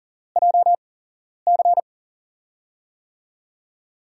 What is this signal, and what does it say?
Morse code, heard as a keyed tone, about 38 wpm, spaced as for 11 wpm, 700 Hz. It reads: JC
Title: USA Call Signs - 8x Character Spacing - 40wpm